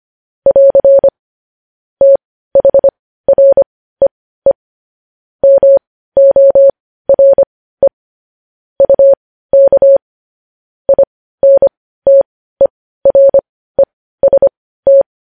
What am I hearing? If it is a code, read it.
Morse code, heard as a keyed tone, about 25 wpm, spaced as for 15 wpm, 550 Hz. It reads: <AR> THREE MORE UK INTEREST
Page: News headlines in morse code at 15 wpm